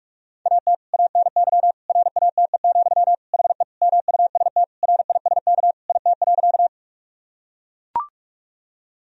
Morse code, heard as a keyed tone, about 43 wpm, 700 Hz. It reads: AT ANY RATE, HE MUST RISK IT.
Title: Sentences from Top 1000 Words - Review Part 4 - 45wpm